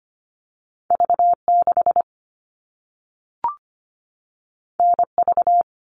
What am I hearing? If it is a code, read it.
Morse code, heard as a keyed tone, about 25 wpm, 700 Hz. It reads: V6 NV
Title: All Cut Numbers - 2-Digit 25wpm